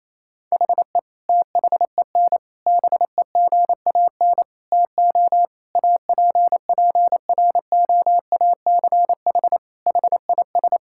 Morse code, heard as a keyed tone, about 28 wpm, 700 Hz. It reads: HE THEN BEGAN TO APPROACH HIS
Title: Sentences from Top 1000 Words - Review Part 6 - 28wpm